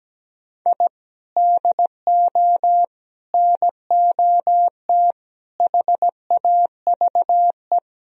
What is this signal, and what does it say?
Morse code, heard as a keyed tone, about 17 wpm, 700 Hz. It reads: I DO NOT HAVE
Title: Sentences from Top 1000 Words - Review Part 3 - 17wpm